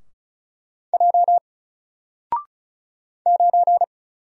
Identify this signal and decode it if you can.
Morse code, heard as a keyed tone, about 36 wpm, 700 Hz. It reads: J 9
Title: Single Character (Letter:Number:Punctuation:Prosign) Alt 35wpm